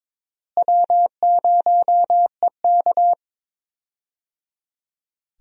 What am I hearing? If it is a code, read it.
Morse code, heard as a keyed tone, about 21 wpm, 700 Hz. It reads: W0EK
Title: USA Call Signs - 4 Characters - Repeated 3x - 22wpm